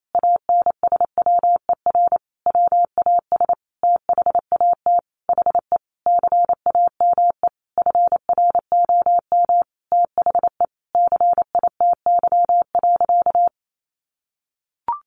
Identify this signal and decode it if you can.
Morse code, heard as a keyed tone, about 28 wpm, 700 Hz. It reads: ANSWER WAS THAT HE CAME FROM THE CITY.
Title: Sentences from Top 1500 Words 28wpm